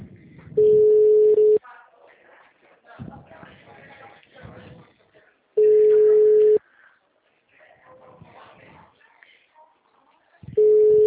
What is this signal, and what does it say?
Ringback tone (Europe, 425 Hz), with speech.